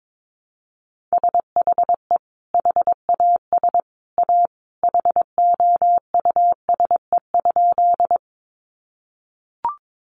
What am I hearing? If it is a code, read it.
Morse code, heard as a keyed tone, about 22 wpm, 700 Hz. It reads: SHE HAS A HOUSE?
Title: Sentences from Top 200 Words 22wpm